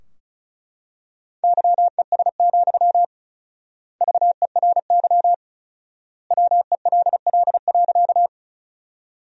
Morse code, heard as a keyed tone, about 36 wpm, 700 Hz. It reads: YES, VERY WELL.
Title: 4x Word-Spacing - Sentences from Top 500 Words - 35wpm